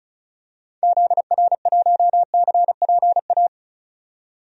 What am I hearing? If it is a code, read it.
Morse code, heard as a keyed tone, about 35 wpm, 700 Hz. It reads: ZR1CPA
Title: DXCC-100 Call Signs - 6 Characters - 35wpm